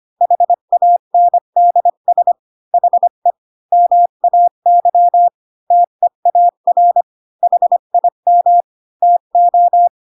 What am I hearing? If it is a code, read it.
Morse code, heard as a keyed tone, about 25 wpm, 700 Hz. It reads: HANDS HE MAY TEAR HIM TO
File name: Pinocchio / Pinocchio ch3 20 25 ik1hge